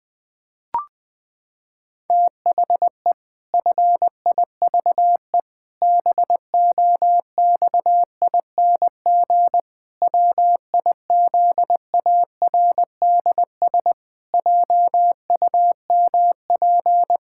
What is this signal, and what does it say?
Morse code, heard as a keyed tone, about 21 wpm, 700 Hz. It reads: THE FIVE BOXING WIZARDS JUMP
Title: Pangram Challenge v2 - 20wpm